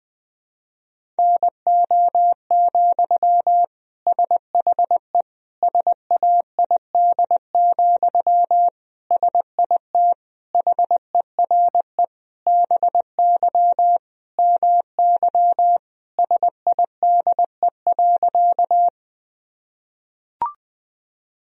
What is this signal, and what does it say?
Morse code, heard as a keyed tone, about 20 wpm, 700 Hz. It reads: NO, SHE SAID, SIT HERE BY MY SIDE.
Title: Sentences from Top 1000 Words - Review Part 1 - 20wpm